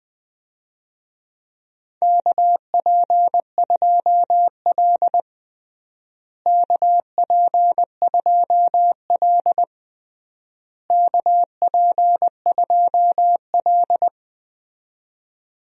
Morse code, heard as a keyed tone, about 20 wpm, 700 Hz. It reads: KP2L KP2L KP2L
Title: DXCC-100 Call Signs - 4 Characters - Repeated 3x - 20wpm